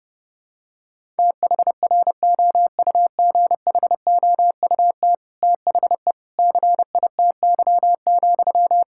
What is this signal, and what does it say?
Morse code, heard as a keyed tone, about 30 wpm, 700 Hz. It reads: THROUGHOUT THE CITY,
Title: Sentences from Top 1000 Words - Review Part 5 - 30wpm